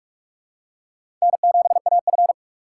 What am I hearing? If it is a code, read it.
Morse code, heard as a keyed tone, about 46 wpm, 700 Hz. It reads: N7AF
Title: USA Call Signs - 3x Character Spacing - 45wpm